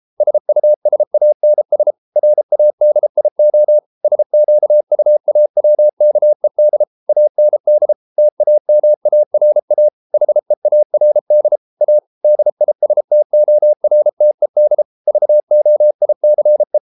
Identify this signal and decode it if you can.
Morse code, heard as a keyed tone, about 33 wpm, 600 Hz. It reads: SUSANS RADIO SQUAWKED AND TAMARA HEARD A DISTORTED VOICE